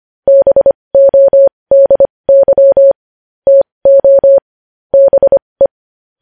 Morse code, heard as a keyed tone, about 25 wpm, spaced as for 20 wpm, 550 Hz. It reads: BODY TO BE
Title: News Headlines at 20